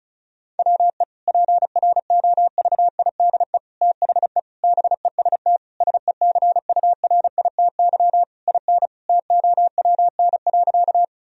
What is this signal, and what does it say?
Morse code, heard as a keyed tone, about 36 wpm, 700 Hz. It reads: WE PROVIDE THE BEST SECURITY IN TOWN.
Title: Sentences from Top 1000 Words - Review Part 4 - 35wpm